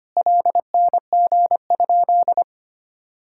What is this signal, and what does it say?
Morse code, heard as a keyed tone, about 25 wpm, 700 Hz. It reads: LNG?
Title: Sentences from Top 500 Words and Abbreviations 25wpm